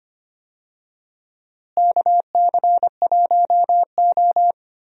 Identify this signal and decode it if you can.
Morse code, heard as a keyed tone, about 25 wpm, 700 Hz. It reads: KC1O
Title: USA Call Signs - 4 Characters - Repeated 2x - 25wpm